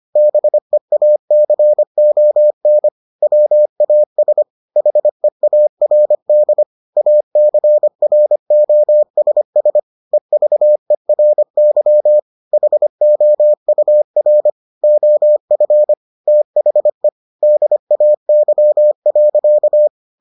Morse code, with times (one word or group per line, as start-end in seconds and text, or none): 0.2-2.9 BEACON
3.2-4.4 WAS
4.8-6.6 HEARD
7.0-9.8 ACROSS
10.1-12.2 EVERY
12.5-14.5 HOUR
14.8-15.9 OF
16.3-17.1 THE
17.4-19.9 DAY.